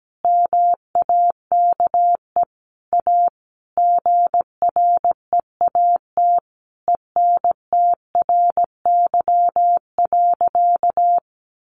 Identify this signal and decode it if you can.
Morse code, heard as a keyed tone, about 17 wpm, 700 Hz. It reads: MAKE A GREAT ENTRY.
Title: Sentences from Top 1500 Words 17wpm